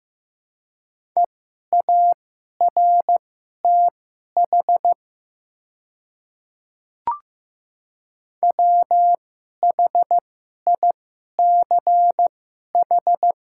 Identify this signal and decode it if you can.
Morse code, heard as a keyed tone, about 15 wpm, spaced as for 11 wpm, 700 Hz. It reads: EARTH WHICH
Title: 2x Character Spacing - Top 200 Words - 15wpm